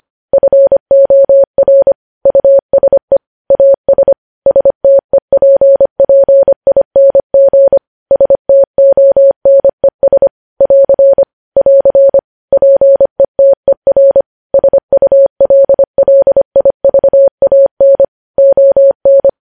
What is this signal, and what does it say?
Morse code, heard as a keyed tone, about 25 wpm, 550 Hz. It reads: FOR USE AS STEPPING STONES <AR> <AR> PETER SULLIVAN ON
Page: News Headlines in Morse Code at 25 WPM